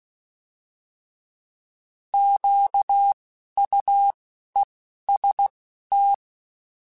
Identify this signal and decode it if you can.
Morse code, heard as a keyed tone, about 16 wpm, 800 Hz. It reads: Q U E S T
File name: Misc / Misc-mp3s / Misc @10